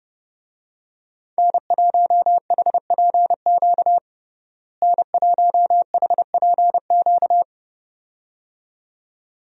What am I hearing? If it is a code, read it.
Morse code, heard as a keyed tone, about 30 wpm, 700 Hz. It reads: N1HPQ N1HPQ
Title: USA Call Signs - 5 Characters - Repeated 2x - 30wpm